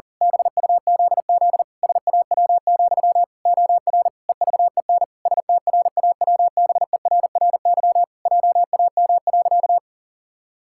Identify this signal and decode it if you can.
Morse code, heard as a keyed tone, about 40 wpm, 700 Hz. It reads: BUZZ SAW, OR EVEN STRAWBERRY JAM.
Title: Pangram Challenge v2 - 40wpm